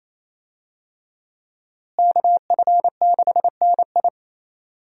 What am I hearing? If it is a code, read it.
Morse code, heard as a keyed tone, about 27 wpm, 700 Hz. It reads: KF6NI
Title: USA Call Signs - 5 Characters - Repeated 3x - 28wpm